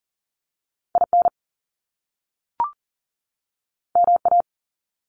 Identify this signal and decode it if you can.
Morse code, heard as a keyed tone, about 42 wpm, 700 Hz. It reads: IN MA
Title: US States 40wpm